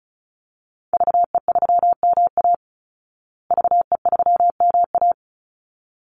Morse code, heard as a keyed tone, about 34 wpm, 700 Hz. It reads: VE3MA VE3MA
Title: Call Signs Repeated 3x - 35wpm